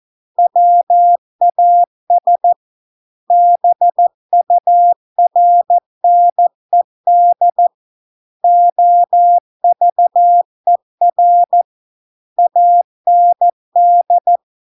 Morse code, heard as a keyed tone, about 14 wpm, 700 Hz. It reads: WAS BURNED OVER AND